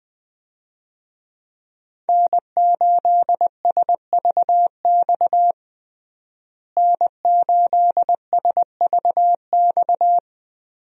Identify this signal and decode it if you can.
Morse code, heard as a keyed tone, about 20 wpm, 700 Hz. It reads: N8SVX N8SVX
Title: USA Call Signs - 5 Characters - Repeated 2x - 20wpm